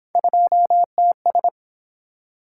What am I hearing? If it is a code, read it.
Morse code, heard as a keyed tone, about 29 wpm, 700 Hz. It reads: 2TS